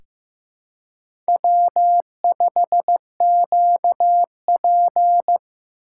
Morse code, heard as a keyed tone, about 15 wpm, 700 Hz. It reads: W5QP
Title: K1USN SST QSO Practice - 15wpm